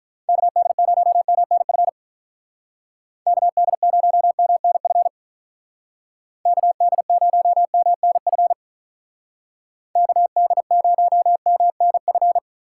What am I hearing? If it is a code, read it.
Morse code, heard as a keyed tone, about 44 wpm, 700 Hz. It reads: KD0MNF KD0MNF KD0MNF KD0MNF